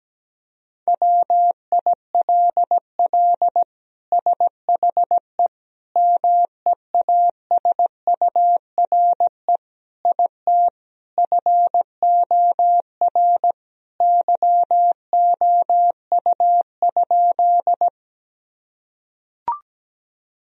Morse code, heard as a keyed tone, about 17 wpm, 700 Hz. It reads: WILL SHE MEASURE IT FOR YOU?
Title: Sentences from Top 1000 Words - Review Part 2 - 17wpm